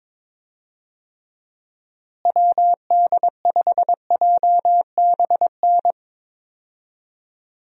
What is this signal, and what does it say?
Morse code, heard as a keyed tone, about 22 wpm, 700 Hz. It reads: WD5JBN
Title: USA Call Signs - 6 Characters - Repeated 2x - 22wpm